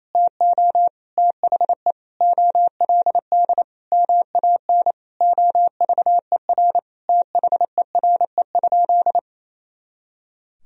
Morse code, heard as a keyed tone, about 28 wpm, 700 Hz. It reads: TO THE OLD MAN OVER THERE?
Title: Sentences from Top 200 Words and Abbreviations 28wpm